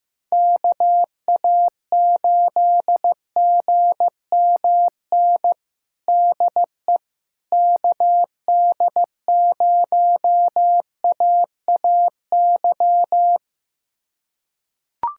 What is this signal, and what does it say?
Morse code, heard as a keyed tone, about 15 wpm, 700 Hz. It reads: KA8GMN DE KD0AAY
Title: USA Callsign DE Callsign 15wpm